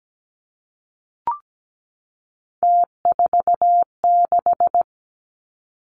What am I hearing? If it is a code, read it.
Morse code, heard as a keyed tone, about 17 wpm, 700 Hz. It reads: T46
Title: Three Challenging Letter-Number Combinations 17wpm